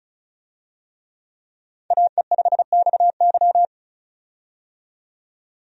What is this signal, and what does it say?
Morse code, heard as a keyed tone, about 36 wpm, 700 Hz. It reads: AE5XY
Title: USA Call Signs - 5 Characters - 35wpm